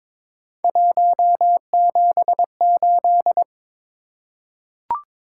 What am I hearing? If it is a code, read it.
Morse code, heard as a keyed tone, about 22 wpm, 700 Hz. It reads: 178
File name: practice / 22wpm / Three Letter-Number Combinations - 22wpm